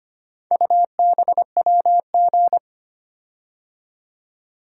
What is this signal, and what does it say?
Morse code, heard as a keyed tone, about 25 wpm, 700 Hz. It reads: UBWG
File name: practice / 25wpm / Four Letter-Number Combinations - 25wpm